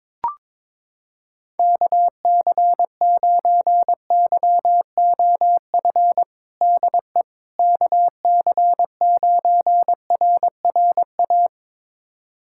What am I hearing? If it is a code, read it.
Morse code, heard as a keyed tone, about 21 wpm, 700 Hz. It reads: KC9YOF DE KC9RRA
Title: USA Callsign DE Callsign 22wpm